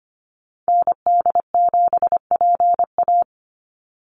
Morse code, heard as a keyed tone, about 25 wpm, 700 Hz. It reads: ND7PA
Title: NA SOTA Activator Callsigns 25wpm